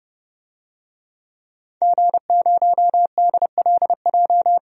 Morse code, heard as a keyed tone, about 30 wpm, 700 Hz. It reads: G0DLJ
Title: DXCC-100 Call Signs - 5 Characters - 30wpm